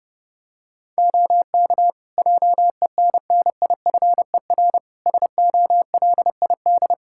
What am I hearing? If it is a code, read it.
Morse code, heard as a keyed tone, about 29 wpm, 700 Hz. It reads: OK JENNIFER SOLID
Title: Practice QSO - Exchange 3 - More Info - 30wpm